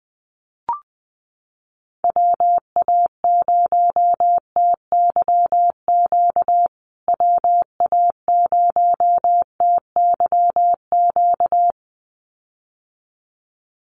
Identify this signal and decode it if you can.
Morse code, heard as a keyed tone, about 20 wpm, 700 Hz. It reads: WA0TYQ WA0TYQ
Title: USA Call Signs Repeated 2x 20wpm